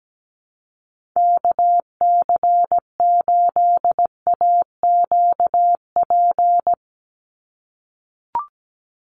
Morse code, heard as a keyed tone, about 17 wpm, 700 Hz. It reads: KC8AQP
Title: USA Call Signs 17wpm